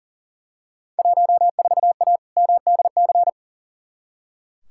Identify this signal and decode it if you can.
Morse code, heard as a keyed tone, about 40 wpm, 700 Hz. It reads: 14A MDC
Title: Field Day Exchanges Encore - 40wpm